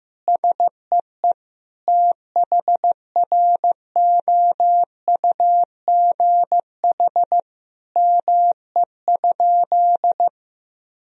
Morse code, heard as a keyed tone, about 15 wpm, 700 Hz. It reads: SEE THROUGH ME?
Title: Sentences from Top 1000 Words - Review Part 1 - 15wpm